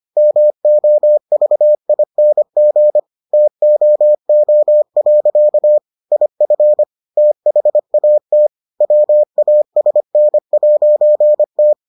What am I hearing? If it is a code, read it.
Morse code, heard as a keyed tone, about 25 wpm, 600 Hz. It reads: MOVING TOO. IF THAT WASN'T